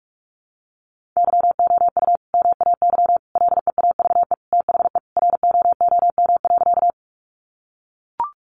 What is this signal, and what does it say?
Morse code, heard as a keyed tone, about 44 wpm, 700 Hz. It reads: YOU MAY LEAVE THE ROOM.
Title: Sentences from Top 400 Words 45wpm